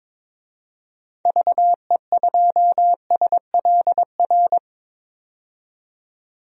Morse code, heard as a keyed tone, about 22 wpm, 700 Hz. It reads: VE2SLR
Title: DXCC-100 Call Signs - 6 Characters - 22wpm